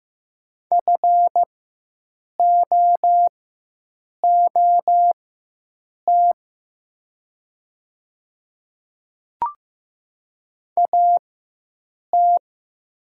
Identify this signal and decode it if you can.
Morse code, heard as a keyed tone, about 15 wpm, spaced as for 7 wpm, 700 Hz. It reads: FOOT AT
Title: 4x Character Spacing - Top 500 Words - 15wpm